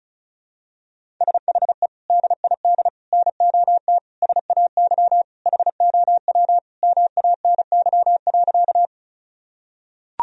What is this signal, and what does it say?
Morse code, heard as a keyed tone, about 36 wpm, 700 Hz. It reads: SHE DID NOT SAY HOW MANY.
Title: Sentences from Top 1000 Words - Review Part 1 - 35wpm